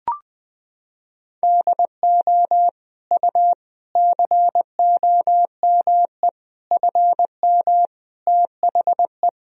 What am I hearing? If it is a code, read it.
Morse code, heard as a keyed tone, about 21 wpm, 700 Hz. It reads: DO U COME FM THE
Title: Sentences from Top 200 Words and Abbreviations 20wpm